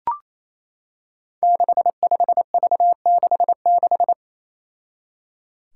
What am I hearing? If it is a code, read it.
Morse code, heard as a keyed tone, about 27 wpm, 700 Hz. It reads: 65V66
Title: Five Challenging Letter-Number Combinations 28wpm